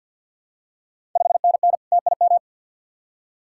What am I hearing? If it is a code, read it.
Morse code, heard as a keyed tone, about 51 wpm, 700 Hz. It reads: 5NN TIM
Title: Ontario QSO Party Exchange 50wpm